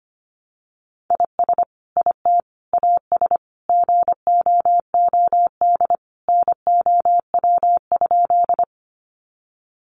Morse code, heard as a keyed tone, about 25 wpm, 700 Hz. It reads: IS IT AS GOOD NOW?
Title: Sentences from Top 200 Words 25wpm